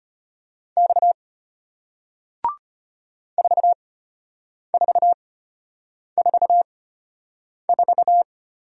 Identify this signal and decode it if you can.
Morse code, heard as a keyed tone, about 33 wpm, spaced as for 8 wpm, 700 Hz. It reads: X 4444